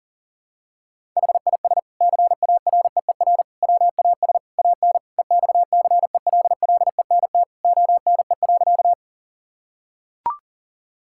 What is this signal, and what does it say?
Morse code, heard as a keyed tone, about 40 wpm, 700 Hz. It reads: HIS CAREER WAS AN EXCELLENT ONE.
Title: Sentences from Top 1000 Words - Review Part 6 - 40wpm